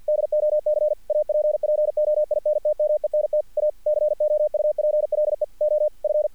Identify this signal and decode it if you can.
Morse code, heard as a keyed tone, about 50 wpm, 600 Hz. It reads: DOC APPOINTMENT A COUPLE OF
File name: archived / CRSnet